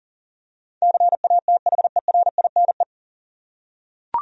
Morse code, heard as a keyed tone, about 37 wpm, 700 Hz. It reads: CATHERINE
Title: USA Common Names - 40wpm